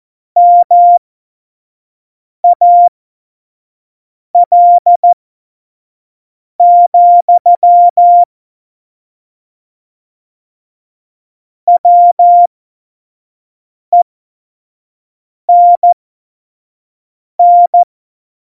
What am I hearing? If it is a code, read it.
Morse code, heard as a keyed tone, about 14 wpm, spaced as for 5 wpm, 700 Hz. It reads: MAL, WENN
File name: Tempo 25 F70